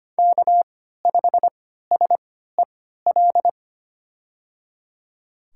Morse code, heard as a keyed tone, about 25 wpm, spaced as for 14 wpm, 700 Hz. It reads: K5SEL
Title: USA Call Signs - 3x Character Spacing - 25wpm